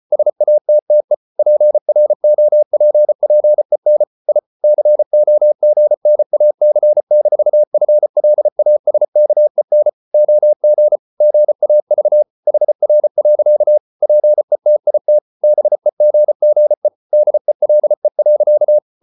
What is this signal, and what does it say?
Morse code, heard as a keyed tone, about 34 wpm, 600 Hz. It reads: SATTE PROPPEN I COGNAC-FLASKEN OG GAV HR. PETIT BEGGE DELE.
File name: Ferie QRQ 34wpm